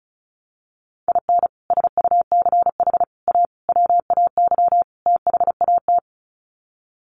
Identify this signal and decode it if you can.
Morse code, heard as a keyed tone, about 34 wpm, 700 Hz. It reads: IN SUCH A WAY THAT
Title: Sets of 5 Words 35wpm